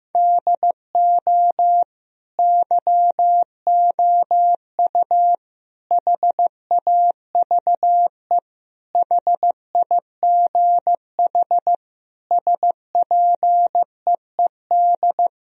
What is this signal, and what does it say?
Morse code, heard as a keyed tone, about 15 wpm, 700 Hz. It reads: DO YOU HAVE HIGH SPEED
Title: Sentences from Top 1000 Words - Review Part 4 - 15wpm